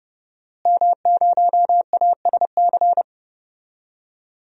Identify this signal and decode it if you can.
Morse code, heard as a keyed tone, about 30 wpm, 700 Hz. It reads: M0ASC
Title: DXCC-100 Call Signs - 5 Characters - Repeated 2x - 30wpm